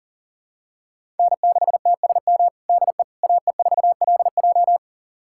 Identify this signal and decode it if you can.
Morse code, heard as a keyed tone, about 37 wpm, 700 Hz. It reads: N6TSM DE AE4LJ
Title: USA Callsign DE Callsign - 5 Characters - 40wpm